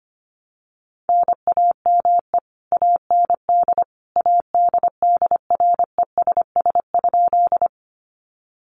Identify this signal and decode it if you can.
Morse code, heard as a keyed tone, about 25 wpm, 700 Hz. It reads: NAME AND ADDRESS?
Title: Sentences from Top 1000 Words 25wpm